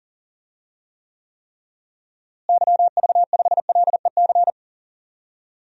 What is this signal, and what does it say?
Morse code, heard as a keyed tone, about 40 wpm, 700 Hz. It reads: YV5LEC
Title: DXCC-100 Call Signs - 6 Characters - Repeated 3x - 40wpm